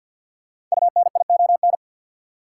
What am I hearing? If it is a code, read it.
Morse code, heard as a keyed tone, about 49 wpm, 700 Hz. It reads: UNION
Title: Sentences from Top 1400 Words - New Word Drill Alt - 50wpm